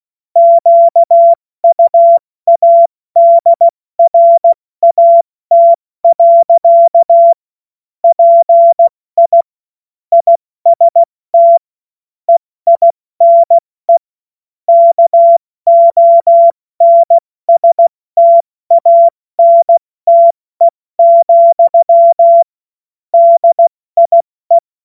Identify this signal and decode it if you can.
Morse code, heard as a keyed tone, about 16 wpm, spaced as for 14 wpm, 680 Hz. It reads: QUADRAT. PI IST EINE KONSTANTE, DIE
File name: Morsekurs / Witze / Witz 21. Juni 2022 70 F80